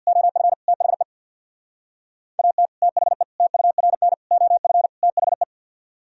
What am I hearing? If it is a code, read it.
Morse code, heard as a keyed tone, about 64 wpm, 700 Hz. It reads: OF THE AT THE TURN OF THE